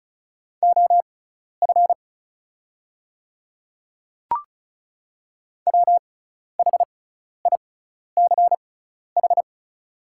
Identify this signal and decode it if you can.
Morse code, heard as a keyed tone, about 36 wpm, spaced as for 12 wpm, 690 Hz. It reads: OF WHICH
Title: 6x Character Spacing - Top 500 Words - 35wpm